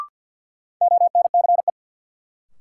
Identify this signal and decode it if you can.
Morse code, heard as a keyed tone, about 47 wpm, 700 Hz. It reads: ONCE
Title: Sentences from Top 300 Words - New Word Drill - 50wpm